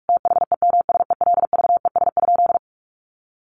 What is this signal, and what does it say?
Morse code, heard as a keyed tone, about 44 wpm, 700 Hz. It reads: THEMSELVES?
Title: Sentences from Top 800 Words 45wpm